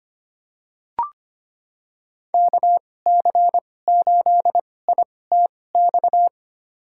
Morse code, heard as a keyed tone, about 25 wpm, spaced as for 18 wpm, 700 Hz. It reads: KC8ITX
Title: USA Call Signs - 6 Characters - 2x Character Spacing - 25wpm